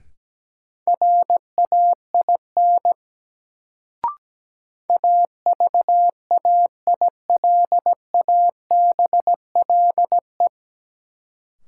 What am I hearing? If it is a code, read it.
Morse code, heard as a keyed tone, about 17 wpm, 700 Hz. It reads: RAIN AVAILABLE
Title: Sentences from Top 600 Words - New Word Drill - 17wpm